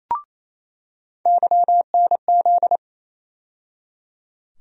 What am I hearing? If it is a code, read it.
Morse code, heard as a keyed tone, about 28 wpm, 700 Hz. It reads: YNZ
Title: USA Call Sign Postfixes - 3 Letters 28wpm